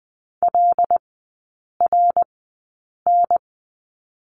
Morse code, heard as a keyed tone, about 20 wpm, spaced as for 8 wpm, 700 Hz. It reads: LRN
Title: 074 Lesson 13e - Letter L - Three Characters 20wpm